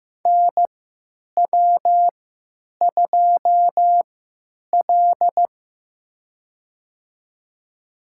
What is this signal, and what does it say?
Morse code, heard as a keyed tone, about 15 wpm, spaced as for 9 wpm, 700 Hz. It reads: NW2L
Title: USA Call Signs - 4 Characters - 3x Character Spacing - 15wpm